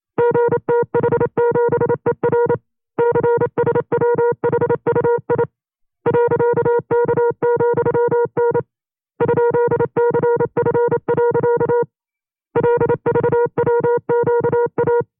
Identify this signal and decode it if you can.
Morse code, heard as a keyed tone, about 28 wpm, 480 Hz. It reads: GTH7ER CSWHUI .K,N ?CF. LVWQA
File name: 25WPM